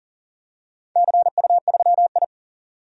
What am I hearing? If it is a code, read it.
Morse code, heard as a keyed tone, about 37 wpm, 700 Hz. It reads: CU3I
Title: DXCC-100 Call Signs - 4 Characters - 40wpm